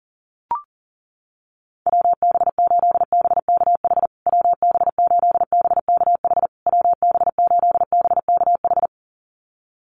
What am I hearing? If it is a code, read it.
Morse code, heard as a keyed tone, about 40 wpm, 700 Hz. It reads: WB8BKH WB8BKH WB8BKH
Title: USA Call Signs Repeated 3x 40wpm